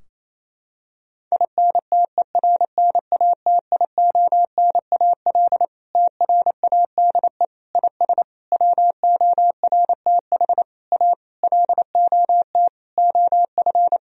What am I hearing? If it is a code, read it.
Morse code, heard as a keyed tone, about 28 wpm, 700 Hz. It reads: INTERNATIONAL TRADE IS WORTH A LOT OF
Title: Sentences from Top 1000 Words - Review Part 4 - 28wpm